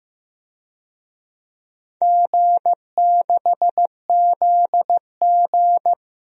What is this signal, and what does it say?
Morse code, heard as a keyed tone, about 15 wpm, 700 Hz. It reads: G6ZG
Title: DXCC-100 Call Signs - 4 Characters - Repeated 3x - 15wpm